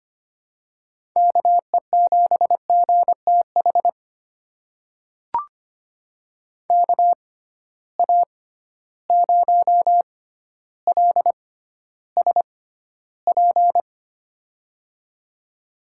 Morse code, heard as a keyed tone, about 25 wpm, spaced as for 13 wpm, 700 Hz. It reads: KE7GTH KA0LSP
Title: USA Call Signs - 6 Characters - 6x Character Spacing - 25wpm